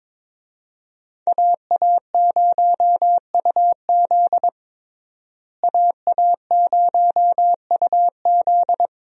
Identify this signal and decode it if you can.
Morse code, heard as a keyed tone, about 22 wpm, 700 Hz. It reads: AA0UZ AA0UZ
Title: USA Call Signs - 5 Characters - Repeated 2x - 22wpm